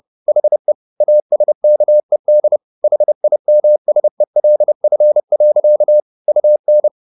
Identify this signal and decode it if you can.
Morse code, heard as a keyed tone, about 30 wpm, 600 Hz. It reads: HE ASKED HIMSELF. UN
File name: the old man and the sea / the old man and the sea 1 30wpm